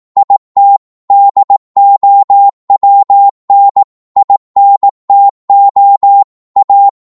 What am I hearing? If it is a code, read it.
Morse code, heard as a keyed tone, about 18 wpm, 800 Hz. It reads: IT DOWN INTO A
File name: 18wpm